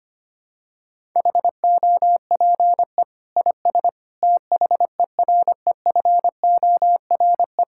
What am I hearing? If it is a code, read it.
Morse code, heard as a keyed tone, about 25 wpm, 700 Hz. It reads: HOPE IS THEREFORE
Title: Sentences from Top 1000 Words - Review Part 5 - 25wpm